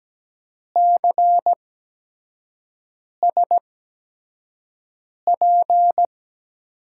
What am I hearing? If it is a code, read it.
Morse code, heard as a keyed tone, about 17 wpm, spaced as for 5 wpm, 700 Hz. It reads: CSP
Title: USA Call Signs - 8x Character Spacing - 17wpm